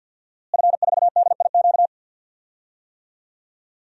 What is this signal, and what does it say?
Morse code, heard as a keyed tone, about 52 wpm, 700 Hz. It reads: F4DIQ
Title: DXCC-100 Call Signs - 5 Characters - Repeated 3x - 50wpm